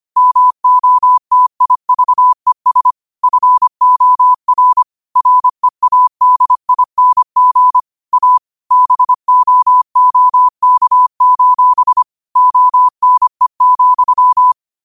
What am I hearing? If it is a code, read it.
Morse code, heard as a keyed tone, about 25 wpm, 1000 Hz. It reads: MOTIVES FOR READING A BOOK: ONE,
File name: qotdfiles / 25wpm